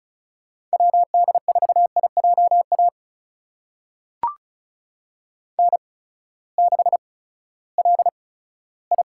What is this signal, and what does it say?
Morse code, heard as a keyed tone, about 36 wpm, spaced as for 18 wpm, 700 Hz. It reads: WD4IJA N6LI
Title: USA Call Signs - 8x Character Spacing - 35wpm